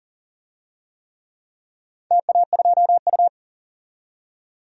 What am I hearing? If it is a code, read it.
Morse code, heard as a keyed tone, about 41 wpm, 700 Hz. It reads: TA2U
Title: DXCC-100 Call Signs - 4 Characters - 40wpm